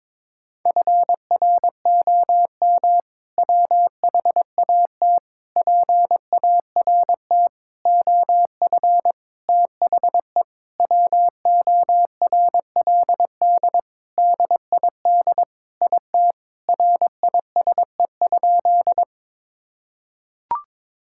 Morse code, heard as a keyed tone, about 22 wpm, 700 Hz. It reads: FROM WHAT PART OF THE WORLD DID IT RISE?
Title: Sentences from Top 1000 Words - Review Part 6 - 22wpm